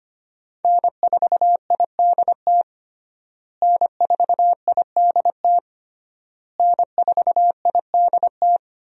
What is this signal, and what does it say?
Morse code, heard as a keyed tone, about 25 wpm, 700 Hz. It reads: N4IDT N4IDT N4IDT
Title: USA Call Signs - 5 Characters - Repeated 3x - 25wpm